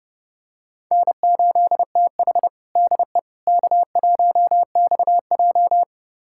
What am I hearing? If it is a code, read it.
Morse code, heard as a keyed tone, about 32 wpm, 700 Hz. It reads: N8TH DE K1XJ
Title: USA Callsign DE Callsign - 4 Characters - 30wpm